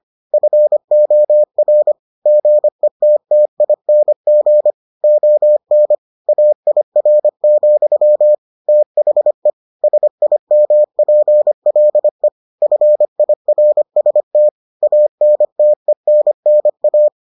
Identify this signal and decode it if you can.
Morse code, heard as a keyed tone, about 25 wpm, 600 Hz. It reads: FOR GETTING ON AIR, THE SIMPLE FIRST ANTENNA